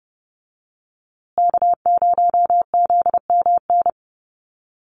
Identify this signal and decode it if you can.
Morse code, heard as a keyed tone, about 30 wpm, 700 Hz. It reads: K0ZMN
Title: USA Call Signs Repeated 2x 30wpm